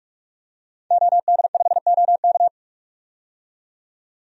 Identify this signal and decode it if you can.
Morse code, heard as a keyed tone, about 42 wpm, 700 Hz. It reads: OD5OK
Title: DXCC-100 Call Signs - 5 Characters - 45wpm